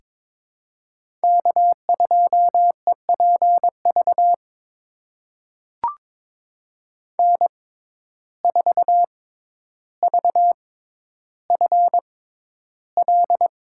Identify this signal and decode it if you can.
Morse code, heard as a keyed tone, about 22 wpm, spaced as for 11 wpm, 700 Hz. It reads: K2EPV N4VFL
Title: USA Call Signs - 5 Characters - 6x Character Spacing - 22wpm